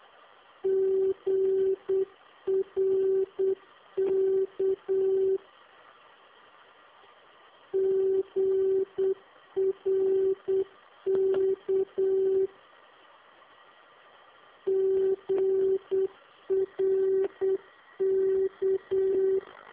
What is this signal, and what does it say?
Morse code, heard as a keyed tone, about 8 wpm, 380 Hz. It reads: GRK GRK GRK